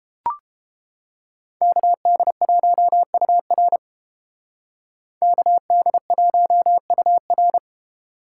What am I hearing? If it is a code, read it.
Morse code, heard as a keyed tone, about 30 wpm, 700 Hz. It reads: KD1UR KD1UR